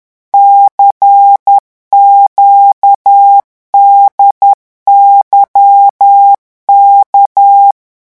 Morse code, heard as a keyed tone, about 10 wpm, 800 Hz. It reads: CQDYK
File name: morse mp3 / revision5-vitesse 10 mots